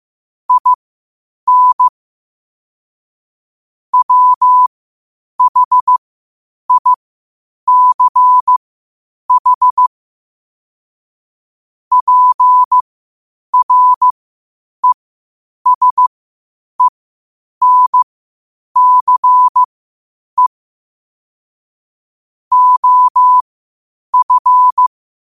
Morse code, heard as a keyed tone, about 15 wpm, spaced as for 8 wpm, 1000 Hz. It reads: IN WHICH PRESENCE OF
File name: qotdfiles / 8wpm